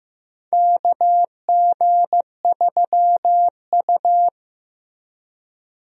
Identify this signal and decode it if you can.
Morse code, heard as a keyed tone, about 15 wpm, 700 Hz. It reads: KG3U
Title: USA Call Signs - 4 Characters - Repeated 2x - 15wpm